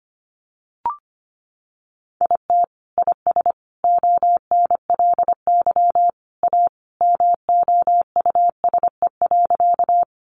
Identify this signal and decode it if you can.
Morse code, heard as a keyed tone, about 25 wpm, 700 Hz. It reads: IT IS ONLY A MOUSE.
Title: Sentences from Top 700 Words 25wpm